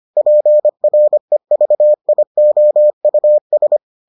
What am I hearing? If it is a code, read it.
Morse code, heard as a keyed tone, about 25 wpm, 600 Hz. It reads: PREVIOUS